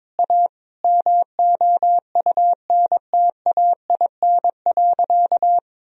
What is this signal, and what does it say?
Morse code, heard as a keyed tone, about 22 wpm, 700 Hz. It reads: A MOUNTAIN.
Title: Sentences from Top 1000 Words - Review Part 3 - 22wpm